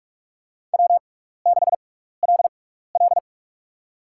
Morse code, heard as a keyed tone, about 45 wpm, spaced as for 15 wpm, 700 Hz. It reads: W6LL
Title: USA Call Signs - 6x Character Spacing - 45wpm